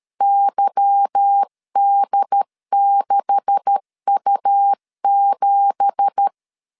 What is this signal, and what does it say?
Morse code, heard as a keyed tone, about 13 wpm, 800 Hz. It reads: YD6U7
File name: LEZIONI / 10 SIGNS / 10B INT 12wpm 2414